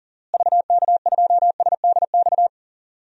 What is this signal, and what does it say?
Morse code, heard as a keyed tone, about 37 wpm, 700 Hz. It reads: VK2SDX
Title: DXCC-100 Call Signs - 6 Characters - Repeated 3x - 40wpm